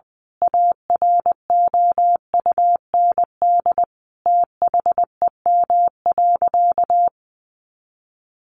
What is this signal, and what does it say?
Morse code, heard as a keyed tone, about 20 wpm, 700 Hz. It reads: AROUND THEM.
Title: A Sound Of Thunder 20wpm